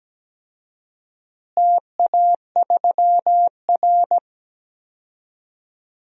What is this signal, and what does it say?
Morse code, heard as a keyed tone, about 17 wpm, 700 Hz. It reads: TA3R
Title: DXCC-100 Call Signs - 4 Characters - Repeated 3x - 17wpm